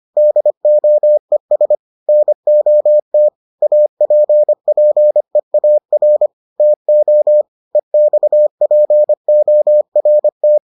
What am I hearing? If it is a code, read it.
Morse code, heard as a keyed tone, about 25 wpm, 600 Hz. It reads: DOES NOT APPEAR TO EXPORT